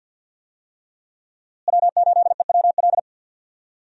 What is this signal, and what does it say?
Morse code, heard as a keyed tone, about 52 wpm, 700 Hz. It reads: W8EWL
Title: USA Call Signs - 5 Characters - Repeated 3x - 50wpm